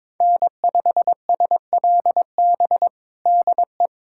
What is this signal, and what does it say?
Morse code, heard as a keyed tone, about 21 wpm, 700 Hz. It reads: N5SLB DE
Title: USA Callsign DE Callsign 22wpm